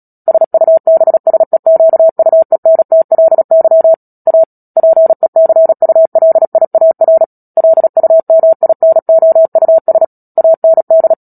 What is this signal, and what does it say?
Morse code, heard as a keyed tone, about 37 wpm, 650 Hz. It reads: SUBSEQUENTLY A PECULIAR LUMINOUS AND
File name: Book Two / 10-Chapter 10 - Epilogue - 36 WPM